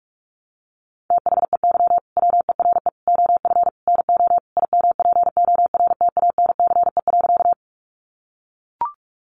Morse code, heard as a keyed tone, about 44 wpm, 700 Hz. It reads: THEY WERE OF NO IMPORTANCE.
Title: Sentences from Top 1200 Words 45wpm